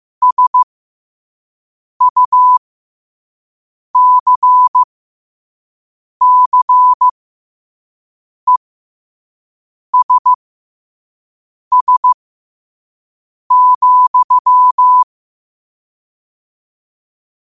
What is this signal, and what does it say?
Morse code, heard as a keyed tone, about 15 wpm, spaced as for 5 wpm, 1000 Hz. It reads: SUCCESS,
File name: qotdfiles / 5wpm